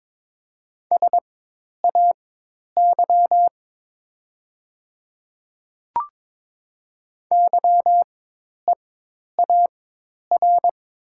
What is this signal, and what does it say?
Morse code, heard as a keyed tone, about 22 wpm, spaced as for 10 wpm, 700 Hz. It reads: SAY YEAR
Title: 4x Character Spacing - Top 200 Words - 22wpm